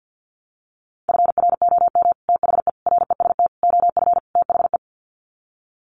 Morse code, heard as a keyed tone, about 50 wpm, 700 Hz. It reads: FROM THE REST OF THE
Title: Sets of 5 Words 50wpm